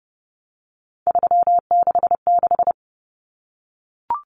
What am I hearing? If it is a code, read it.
Morse code, heard as a keyed tone, about 30 wpm, 700 Hz. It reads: 366
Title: Numbers - 3-Digit 30wpm